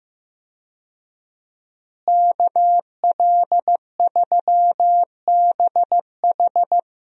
Morse code, heard as a keyed tone, about 15 wpm, 700 Hz. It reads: KL3BH
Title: USA Call Signs - 5 Characters - Repeated 3x - 15wpm